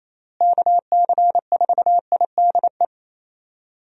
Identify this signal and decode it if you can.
Morse code, heard as a keyed tone, about 28 wpm, 700 Hz. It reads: KC4IDE
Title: USA Call Signs - 6 Characters - Repeated 2x - 28wpm